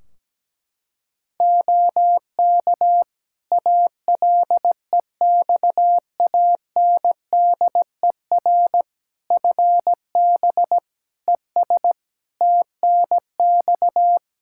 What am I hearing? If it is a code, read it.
Morse code, heard as a keyed tone, about 17 wpm, 700 Hz. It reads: OK ALEXANDER FB ES TNX
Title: Practice QSO - Exchange 2 - Info - 17wpm